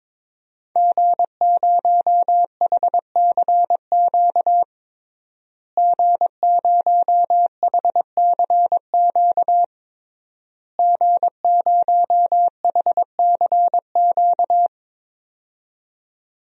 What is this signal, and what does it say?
Morse code, heard as a keyed tone, about 22 wpm, 700 Hz. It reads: G0HCQ G0HCQ G0HCQ
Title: DXCC-100 Call Signs - 5 Characters - Repeated 3x - 22wpm